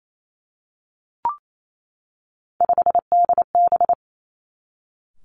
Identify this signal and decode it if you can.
Morse code, heard as a keyed tone, about 28 wpm, 700 Hz. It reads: 5DB
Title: Three Challenging Letter-Number Combinations 28wpm